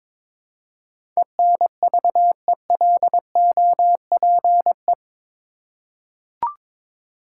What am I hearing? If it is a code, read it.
Morse code, heard as a keyed tone, about 22 wpm, 700 Hz. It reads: ENVELOPE
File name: practice / 22wpm / Eight Letter Words - 22wpm